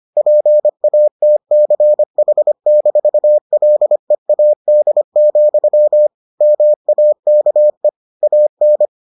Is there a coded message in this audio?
Yes: Morse code from 0.2 to 8.9 s, heard as a keyed tone, about 25 wpm, 600 Hz. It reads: PATCH-LEAD, MAKE AN